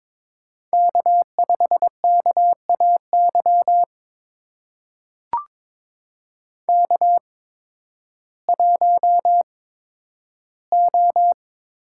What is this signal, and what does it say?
Morse code, heard as a keyed tone, about 22 wpm, spaced as for 12 wpm, 700 Hz. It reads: K5KAY K1O
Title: USA Call Signs - 5 Characters - 8x Character Spacing - 22wpm